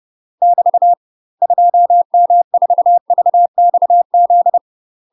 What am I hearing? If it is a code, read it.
Morse code, heard as a keyed tone, about 30 wpm, 700 Hz. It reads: <BT> 2M4VXZ
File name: UK MASTER QSOs TRAINING / 30WPM / UK MASTER QSOs 0011